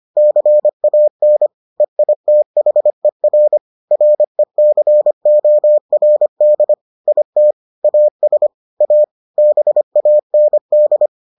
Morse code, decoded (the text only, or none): CAN EITHER RECORD IT AS A BAND